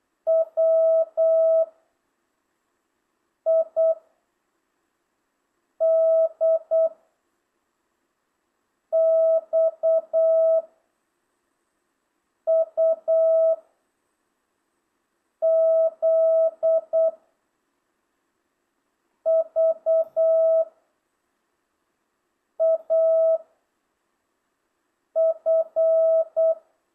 Morse code, heard as a keyed tone, about 8 wpm, 660 Hz. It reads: W I D X U Z V A F